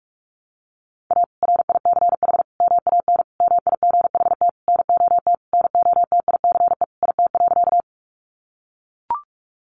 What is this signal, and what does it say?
Morse code, heard as a keyed tone, about 44 wpm, 700 Hz. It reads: A RICH MAN MIGHT NOT NOTICE IT.
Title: Sentences from Top 1100 Words 45wpm